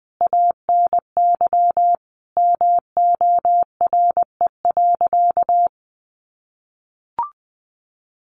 Morse code, heard as a keyed tone, about 20 wpm, 700 Hz. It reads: ANY MORE.
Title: Sentences from Top 1500 Words 20wpm